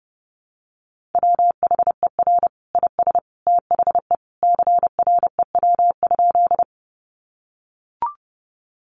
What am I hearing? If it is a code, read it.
Morse code, heard as a keyed tone, about 30 wpm, 700 Hz. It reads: WHER IS THE CREW?
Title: Sentences from Top 1500 Words 30wpm